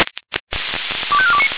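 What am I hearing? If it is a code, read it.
Five-tone selective call (ZVEI), heard as quick runs of five tones, tones of about 70 ms. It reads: 25429
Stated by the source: ZVEI five-tone selective call received on an SDR.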